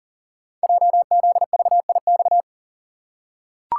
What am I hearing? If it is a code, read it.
Morse code, heard as a keyed tone, about 40 wpm, 700 Hz. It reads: JZVIX
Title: Code Groups - 5 Letters - 40wpm